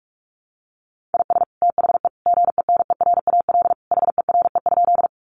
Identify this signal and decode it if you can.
Morse code, heard as a keyed tone, about 44 wpm, 700 Hz. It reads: IS THE GENERAL HERE?
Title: Sentences from Top 600 Words 45wpm